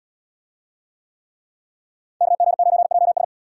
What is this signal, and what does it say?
Morse code, heard as a keyed tone, about 72 wpm, 700 Hz. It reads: KG9JS